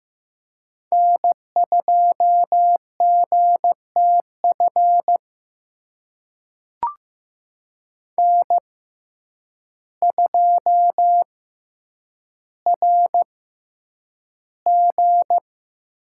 Morse code, heard as a keyed tone, about 15 wpm, spaced as for 8 wpm, 700 Hz. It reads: N2GTF N2RG
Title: USA Call Signs - 5 Characters - 6x Character Spacing - 15wpm